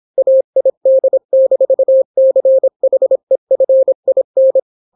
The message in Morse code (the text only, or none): AID-CHEFIN